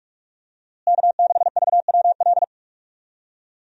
Morse code, heard as a keyed tone, about 45 wpm, 700 Hz. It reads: K6VWL
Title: USA Call Signs - 5 Characters - Repeated 2x - 45wpm